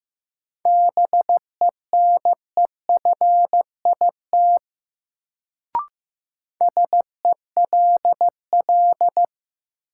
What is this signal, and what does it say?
Morse code, heard as a keyed tone, about 15 wpm, 700 Hz. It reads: BENEFIT SELL
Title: Sentences from Top 1000 Words - Review Part 6 - New Word Drill - 15wpm